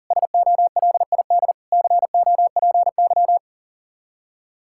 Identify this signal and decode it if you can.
Morse code, heard as a keyed tone, about 40 wpm, 700 Hz. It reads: SOLID COPY
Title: Practice QSO - Exchange 3 - More Info - 40wpm